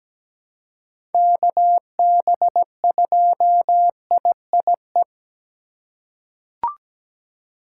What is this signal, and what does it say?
Morse code, heard as a keyed tone, about 17 wpm, 700 Hz. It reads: KB2IIE
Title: USA Call Signs - 6 Characters - 3x Character Spacing - 17wpm